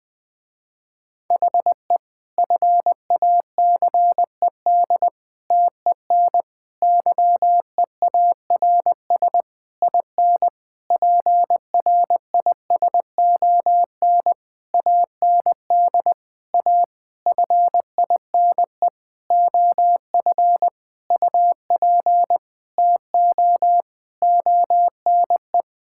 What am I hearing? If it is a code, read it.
Morse code, heard as a keyed tone, about 20 wpm, 700 Hz. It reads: HE FACED TEN YEARS IN PRISON AND A FINE OF UP TO ONE
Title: Sentences with 9-Word Phrases - Spoken Beginning - 20wpm